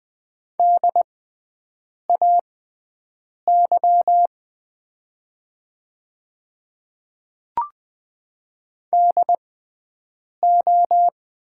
Morse code, heard as a keyed tone, about 20 wpm, spaced as for 7 wpm, 700 Hz. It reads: DAY DO
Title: 6x Character Spacing - Top 100 Words - 20wpm